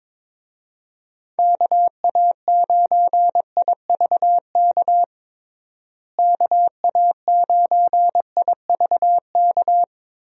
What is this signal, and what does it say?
Morse code, heard as a keyed tone, about 22 wpm, 700 Hz. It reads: KA9IVK KA9IVK
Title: USA Call Signs - 6 Characters - Repeated 2x - 22wpm